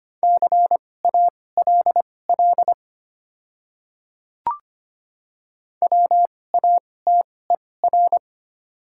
Morse code, heard as a keyed tone, about 25 wpm, spaced as for 18 wpm, 700 Hz. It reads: CALL WATER
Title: 2x Character Spacing - Top 200 Words - 25wpm